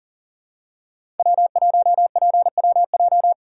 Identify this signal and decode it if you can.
Morse code, heard as a keyed tone, about 40 wpm, 700 Hz. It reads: W1PWJ
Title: USA Call Signs - 5 Characters - 6x Character Spacing - 40wpm